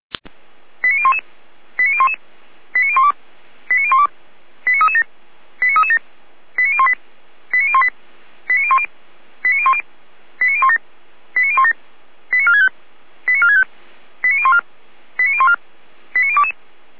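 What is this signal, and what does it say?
Five-tone selective call (ZVEI), heard as quick runs of five tones, tones of about 70 ms. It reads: 89011 89011 89012 89012 89337 89337 89018 89018 89010 89010 89017 89017 89465 89465 89014 89014 89022
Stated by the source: Received on an SDR from server audio.f85360.feuerwehr-orlamuende.org (Feuerwehr Orlamünde fire brigade).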